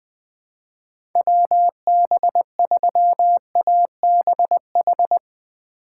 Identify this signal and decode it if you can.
Morse code, heard as a keyed tone, about 21 wpm, 700 Hz. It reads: WB3ABH
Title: USA Call Signs - 6 Characters - 4x Character Spacing - 20wpm